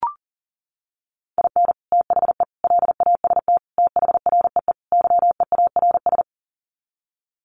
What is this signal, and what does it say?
Morse code, heard as a keyed tone, about 40 wpm, 700 Hz. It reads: IN THE LAST THREE YEARS
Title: Sets of 5 Words Encore2 40wpm